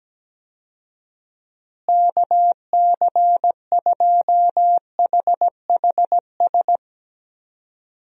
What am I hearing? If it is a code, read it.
Morse code, heard as a keyed tone, about 17 wpm, 700 Hz. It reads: KC2HHS
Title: USA Call Signs - 6 Characters - Repeated 2x - 17wpm